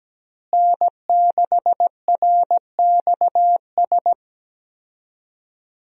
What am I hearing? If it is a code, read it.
Morse code, heard as a keyed tone, about 17 wpm, 700 Hz. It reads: N6RXS
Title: USA Call Signs - 5 Characters - Repeated 3x - 17wpm